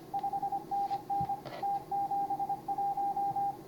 Morse code, heard as a keyed tone, about 26 wpm, 810 Hz, with radio noise. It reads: 5NN T71